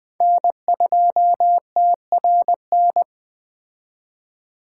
Morse code, heard as a keyed tone, about 20 wpm, 700 Hz. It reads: N2TRN
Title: Speed Racing - USA 1x3 Call Signs 17wpm